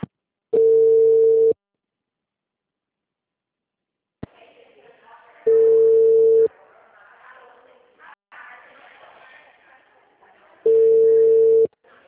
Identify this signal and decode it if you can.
Ringback tone (China, 450 Hz), with speech.